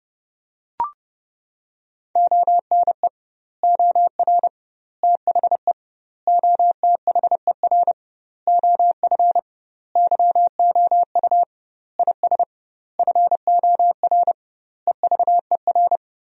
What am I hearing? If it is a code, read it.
Morse code, heard as a keyed tone, about 30 wpm, 700 Hz. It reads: ONE OR THE OTHER OF YOU IS FOR EVER
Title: 2x Word-Spacing - Sentences from Top 400 Words - 30wpm